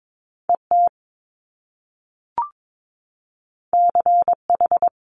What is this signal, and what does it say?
Morse code, heard as a keyed tone, about 22 wpm, 700 Hz. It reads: E T C H
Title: Top Two Letter Combinations in Words 22wpm